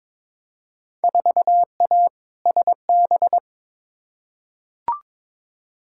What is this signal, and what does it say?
Morse code, heard as a keyed tone, about 22 wpm, 700 Hz. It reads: 4A SB
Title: Field Day Exchanges Encore - 22wpm